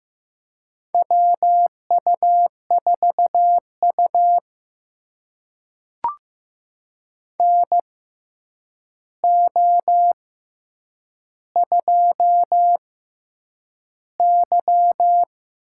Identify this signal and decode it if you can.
Morse code, heard as a keyed tone, about 15 wpm, spaced as for 8 wpm, 700 Hz. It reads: WU4U NO2Y
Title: USA Call Signs - 4 Characters - 6x Character Spacing - 15wpm